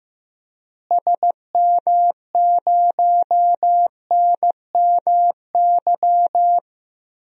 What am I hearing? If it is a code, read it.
Morse code, heard as a keyed tone, about 15 wpm, 700 Hz. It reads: SM0NMY
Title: DXCC-100 Call Signs - 6 Characters - Repeated 2x - 15wpm